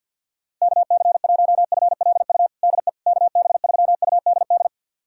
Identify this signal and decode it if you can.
Morse code, heard as a keyed tone, about 51 wpm, 700 Hz. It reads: KC1FLU DE KB3UDD
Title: USA Callsign DE Callsign - 6 Characters - 50wpm